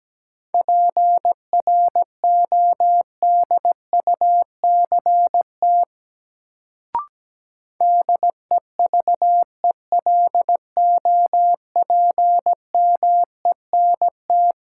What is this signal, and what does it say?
Morse code, heard as a keyed tone, about 17 wpm, 700 Hz. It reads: PRODUCT DEVELOPMENT
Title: Sentences from Top 1400 Words - New Word Drill Alt - 17wpm